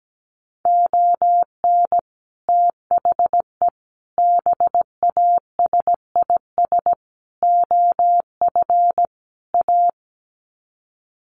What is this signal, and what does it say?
Morse code, heard as a keyed tone, about 17 wpm, 700 Hz. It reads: ON THE BASIS OF A
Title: Sets of 5 Words Encore 17wpm